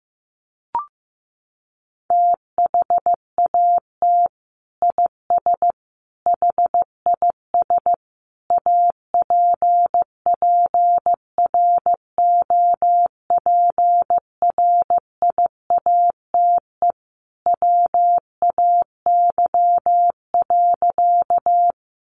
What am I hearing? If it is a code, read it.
Morse code, heard as a keyed tone, about 15 wpm, 700 Hz. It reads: THAT IS HIS APPROPRIATE WAY.
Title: Sentences from Top 900 Words 15wpm